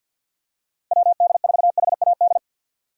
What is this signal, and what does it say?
Morse code, heard as a keyed tone, about 52 wpm, 700 Hz. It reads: WD4HAD
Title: USA Call Signs - 6 Characters - Repeated 3x - 50wpm